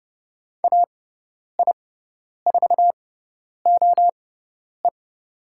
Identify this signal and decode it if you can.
Morse code, heard as a keyed tone, about 30 wpm, spaced as for 10 wpm, 700 Hz. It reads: AI4OE
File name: Lesson 18g - Letter m - Call Signs 30wpm